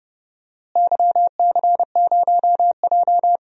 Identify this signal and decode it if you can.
Morse code, heard as a keyed tone, about 30 wpm, 700 Hz. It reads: YC0J
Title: DXCC-100 Call Signs - 4 Characters - Repeated 3x - 30wpm